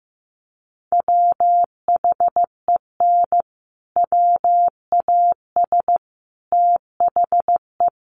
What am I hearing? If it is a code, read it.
Morse code, heard as a keyed tone, about 14 wpm, 700 Hz. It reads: WHEN WAS THE
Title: Sets of 5 Words Encore 15wpm